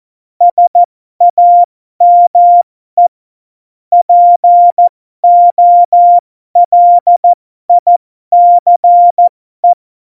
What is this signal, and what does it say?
Morse code, heard as a keyed tone, about 14 wpm, spaced as for 12 wpm, 700 Hz. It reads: SAME POLICE